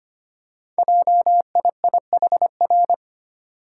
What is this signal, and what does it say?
Morse code, heard as a keyed tone, about 25 wpm, 700 Hz. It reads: JIIHR
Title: Code Groups - 5 Letters - 25wpm